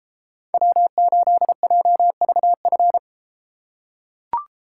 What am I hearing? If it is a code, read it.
Morse code, heard as a keyed tone, about 33 wpm, 700 Hz. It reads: W8JVF